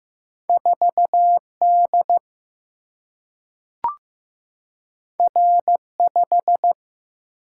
Morse code, heard as a keyed tone, about 15 wpm, 700 Hz. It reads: 4 D R 5
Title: Single-Letter Single-Number 15wpm